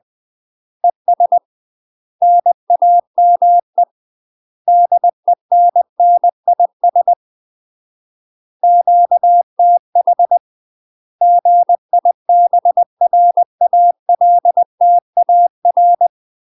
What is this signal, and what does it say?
Morse code, heard as a keyed tone, about 20 wpm, 700 Hz. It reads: ES NAME DENNIS QTH GIBRALTAR
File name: UK MASTER QSOs TRAINING/20WPM